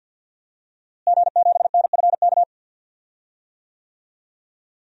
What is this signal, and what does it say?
Morse code, heard as a keyed tone, about 50 wpm, 700 Hz. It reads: G7NFK
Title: DXCC-100 Call Signs - 5 Characters - Repeated 3x - 50wpm